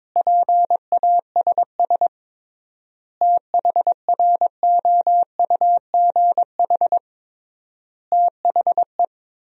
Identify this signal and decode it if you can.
Morse code, heard as a keyed tone, about 22 wpm, 700 Hz. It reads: PASS THROUGH THE
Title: 3x Word-Spacing - Sentences from Top 400 Words - 22wpm